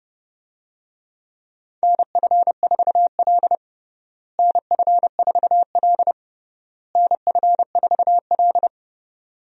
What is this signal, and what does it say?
Morse code, heard as a keyed tone, about 30 wpm, 700 Hz. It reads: NF4L NF4L NF4L
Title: USA Call Signs - 4 Characters - Repeated 3x - 30wpm